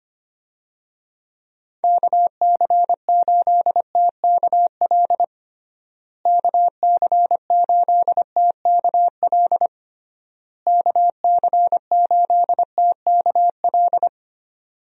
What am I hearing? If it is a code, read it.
Morse code, heard as a keyed tone, about 25 wpm, 700 Hz. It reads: KC8TKL KC8TKL KC8TKL
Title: USA Call Signs - 6 Characters - Repeated 3x - 25wpm